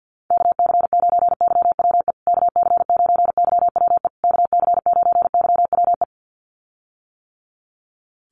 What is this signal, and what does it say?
Morse code, heard as a keyed tone, about 50 wpm, 700 Hz. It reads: KC9YWE KC9YWE KC9YWE
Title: USA Call Signs Repeated 3x 50wpm